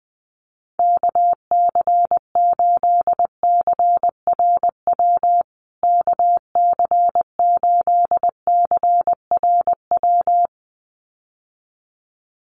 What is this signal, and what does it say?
Morse code, heard as a keyed tone, about 20 wpm, 700 Hz. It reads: KC8CRW KC8CRW
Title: USA Call Signs Repeated 2x 20wpm